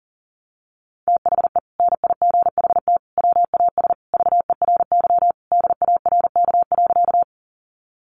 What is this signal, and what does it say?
Morse code, heard as a keyed tone, about 40 wpm, 700 Hz. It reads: THE NIGHT WAS VERY DARK.
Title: Sentences from Top 500 Words 40wpm